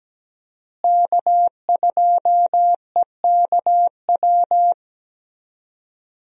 Morse code, heard as a keyed tone, about 17 wpm, 700 Hz. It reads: K2EKW
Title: USA Call Signs - 5 Characters - Repeated 3x - 17wpm